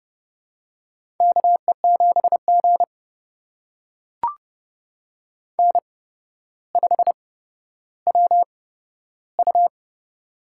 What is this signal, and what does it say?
Morse code, heard as a keyed tone, about 30 wpm, spaced as for 13 wpm, 700 Hz. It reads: KE7G N5WU
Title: USA Call Signs - 8x Character Spacing - 30wpm